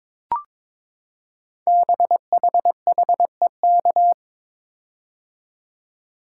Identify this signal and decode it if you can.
Morse code, heard as a keyed tone, about 23 wpm, 700 Hz. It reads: BHHEK
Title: Code Groups - 5 Letters - 22wpm